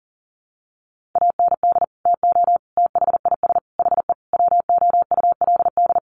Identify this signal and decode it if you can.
Morse code, heard as a keyed tone, about 40 wpm, 700 Hz. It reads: AND TO THIS HE WOULD
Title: Sentences from Top 1200 Words 40wpm